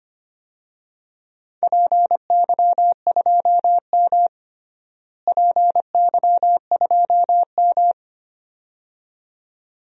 Morse code, heard as a keyed tone, about 25 wpm, 700 Hz. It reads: PY2M PY2M
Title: DXCC-100 Call Signs - 4 Characters - Repeated 2x - 25wpm